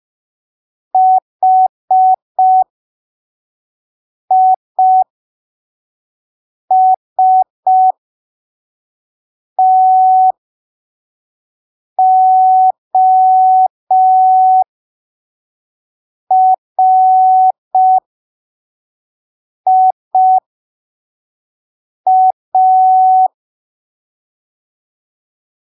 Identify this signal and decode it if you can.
Morse code, heard as a keyed tone, about 5 wpm, spaced as for 3 wpm, 750 Hz. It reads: HISTORIA
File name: arquivos / texto / texto 05 5ppm